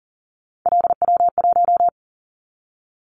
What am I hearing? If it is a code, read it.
Morse code, heard as a keyed tone, about 40 wpm, 700 Hz. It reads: LW1
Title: Callsign Prefixes for Top 50 DXCC - 40wpm